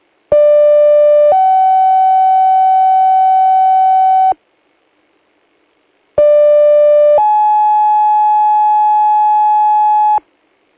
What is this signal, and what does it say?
Two-tone page, heard as a one-second tone, then a three-second tone, about 590 then 760 Hz, about 590 then 850 Hz, with noise. It is heard from 0.3 to 10.2 s.